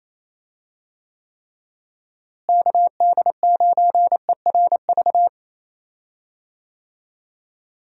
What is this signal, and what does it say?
Morse code, heard as a keyed tone, about 28 wpm, 700 Hz. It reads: KD9ERV
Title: USA Call Signs - 6 Characters - Repeated 3x - 28wpm